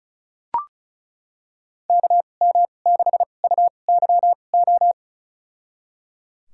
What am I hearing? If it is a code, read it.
Morse code, heard as a keyed tone, about 36 wpm, spaced as for 25 wpm, 700 Hz. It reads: KM6UYO
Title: USA Call Signs - 6 Characters - 2x Character Spacing - 35wpm